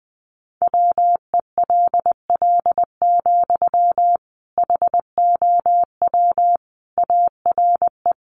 Morse code, heard as a keyed tone, about 20 wpm, 700 Hz. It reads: WELL, HOW ARE
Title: Sentences from Top 200 Words 20wpm